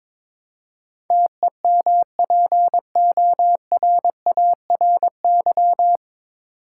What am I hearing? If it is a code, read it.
Morse code, heard as a keyed tone, about 22 wpm, 700 Hz. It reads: TEMPORARY
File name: 9 Letter Words - First 3 Letters Given - 22wpm